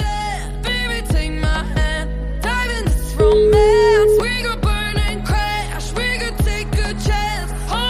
Ringback tone (Europe, 425 Hz), with noise.